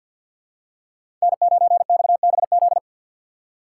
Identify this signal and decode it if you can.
Morse code, heard as a keyed tone, about 51 wpm, 700 Hz. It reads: N9XBZ
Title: USA Call Signs - 2x Character Spacing - 50wpm